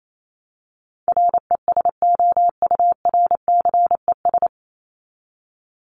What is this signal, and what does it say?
Morse code, heard as a keyed tone, about 28 wpm, 700 Hz. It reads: RESOURCES
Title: ES:S Suffix - Top 100 Words with Base Words 28wpm